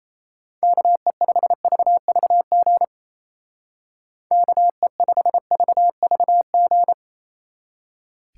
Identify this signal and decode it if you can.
Morse code, heard as a keyed tone, about 33 wpm, 700 Hz. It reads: KE5VVG KE5VVG
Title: Speed Racing - USA 2x3 Call Signs 28wpm